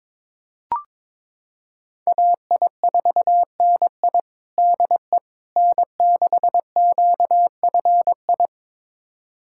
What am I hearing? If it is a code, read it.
Morse code, heard as a keyed tone, about 22 wpm, 700 Hz. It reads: AI4NI DE N6QFI
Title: USA Callsign DE Callsign - 5 Characters - 22wpm